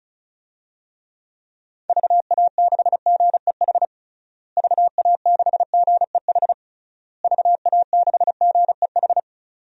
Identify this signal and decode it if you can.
Morse code, heard as a keyed tone, about 34 wpm, 700 Hz. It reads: VA6GEH VA6GEH VA6GEH
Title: DXCC-100 Call Signs - 6 Characters - Repeated 3x - 35wpm